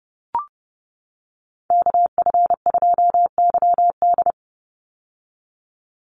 Morse code, heard as a keyed tone, about 30 wpm, 700 Hz. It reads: KF2YD
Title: USA Call Signs 30wpm